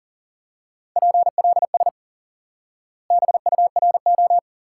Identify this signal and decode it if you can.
Morse code, heard as a keyed tone, about 37 wpm, 700 Hz. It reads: PLS BURO
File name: IQE Course 3 - Lesson 89 - YAGI - Two Elements - 40wpm